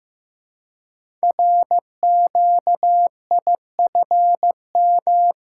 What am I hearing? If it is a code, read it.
Morse code, heard as a keyed tone, about 15 wpm, 700 Hz. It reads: RQIFM
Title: Code Groups - 5 Letters - 15wpm